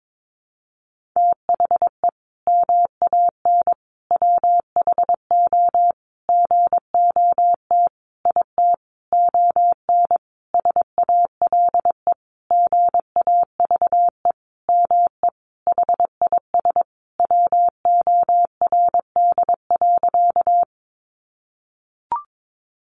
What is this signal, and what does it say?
Morse code, heard as a keyed tone, about 22 wpm, 700 Hz. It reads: THE MAN WHO GOT IT ON SALE GAVE ME HIS WORD.
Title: Sentences from Top 1400 Words 22wpm